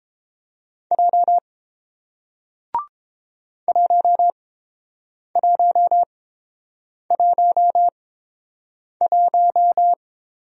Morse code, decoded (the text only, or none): J 1111